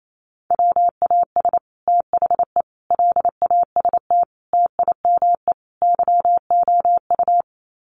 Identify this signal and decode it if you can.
Morse code, heard as a keyed tone, about 28 wpm, 700 Hz. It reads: WAS THE LAST TIME YOU
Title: Sets of 5 Words Encore2 28wpm